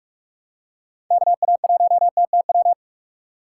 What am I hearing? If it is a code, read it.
Morse code, heard as a keyed tone, about 44 wpm, 700 Hz. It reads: KA1TTW
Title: USA Call Signs - 6 Characters - 2x Character Spacing - 45wpm